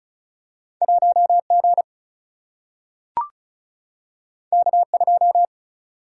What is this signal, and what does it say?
Morse code, heard as a keyed tone, about 36 wpm, 700 Hz. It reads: 1 G K 2
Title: Single-Letter Single-Number 35wpm